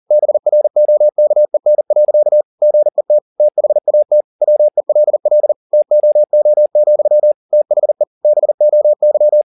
Morse code, heard as a keyed tone, about 40 wpm, 600 Hz. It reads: BROKEN. GET THAT WELL TOO, THE BOY
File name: the old man and the sea 2 40wpm